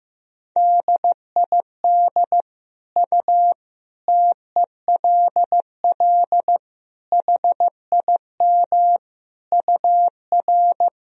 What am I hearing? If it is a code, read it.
Morse code, heard as a keyed tone, about 15 wpm, 700 Hz. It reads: DID U TELL HIM UR
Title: Sentences from Top 400 Words and Abbreviations 15wpm